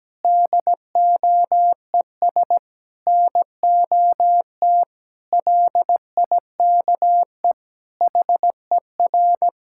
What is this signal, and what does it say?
Morse code, heard as a keyed tone, about 17 wpm, 700 Hz. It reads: DOES NOT LIKE HER
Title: Sentences from Top 1000 Words - Review Part 5 - 17wpm